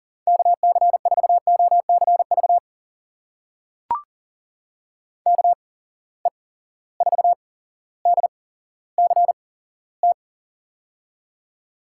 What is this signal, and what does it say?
Morse code, heard as a keyed tone, about 39 wpm, spaced as for 17 wpm, 700 Hz. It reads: KC4OCV KE4DCT
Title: USA Call Signs - 6 Characters - 8x Character Spacing - 40wpm